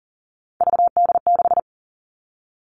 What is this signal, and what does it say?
Morse code, heard as a keyed tone, about 42 wpm, 700 Hz. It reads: VD6
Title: Three Challenging Letter-Number Combinations 40wpm